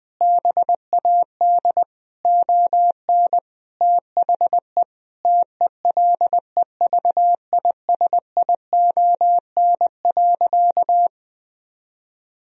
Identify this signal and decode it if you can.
Morse code, heard as a keyed tone, about 20 wpm, 700 Hz. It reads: BAD ON THE TELEVISION.
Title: Sentences from Top 1000 Words - Review Part 4 - 20wpm